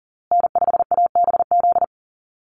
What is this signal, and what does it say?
Morse code, heard as a keyed tone, about 40 wpm, 700 Hz. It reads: N5ABZ
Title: USA Call Signs Repeated 2x 40wpm